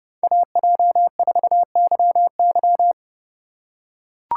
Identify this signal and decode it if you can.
Morse code, heard as a keyed tone, about 31 wpm, 700 Hz. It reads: AJ4YY
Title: USA Call Signs - 5 Characters - 3x Character Spacing - 30wpm